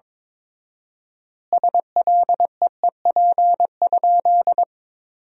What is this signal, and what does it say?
Morse code, heard as a keyed tone, about 22 wpm, 700 Hz. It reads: SLEEP?
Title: 4x Word-Spacing - Sentences from Top 500 Words - 22wpm